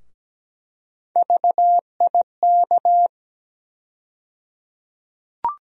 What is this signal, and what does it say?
Morse code, heard as a keyed tone, about 17 wpm, 700 Hz. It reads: VIK
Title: USA Call Sign Postfixes - 3 Letters 17wpm